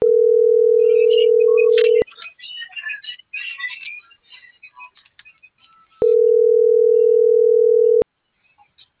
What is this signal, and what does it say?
Ringback tone (North America), 440 + 480 Hz.